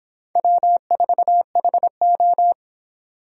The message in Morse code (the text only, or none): W4HO